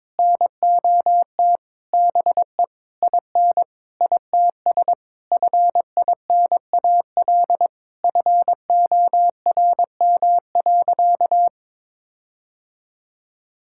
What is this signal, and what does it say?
Morse code, heard as a keyed tone, about 22 wpm, 700 Hz. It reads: NOT BE IN ITS FINAL FORM.
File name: Sentences with 7-Word Phrases - 22wpm